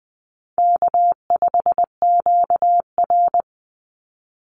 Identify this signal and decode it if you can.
Morse code, heard as a keyed tone, about 20 wpm, 690 Hz. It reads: K5QR
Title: NA SOTA Chaser Callsigns 20wpm